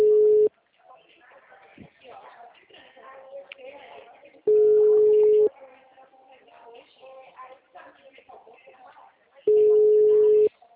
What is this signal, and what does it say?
Ringback tone (Europe, 425 Hz), with speech.
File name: out- -2013- - - .9409